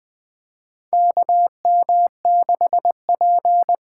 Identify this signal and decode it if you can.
Morse code, heard as a keyed tone, about 20 wpm, 700 Hz. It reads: KM6P
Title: USA Call Signs - 6 Characters - 8x Character Spacing - 20wpm